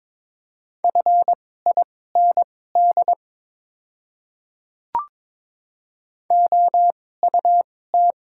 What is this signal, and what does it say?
Morse code, heard as a keyed tone, about 21 wpm, spaced as for 16 wpm, 690 Hz. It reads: FIND OUT
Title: 2x Character Spacing - Top 100 Words - 22wpm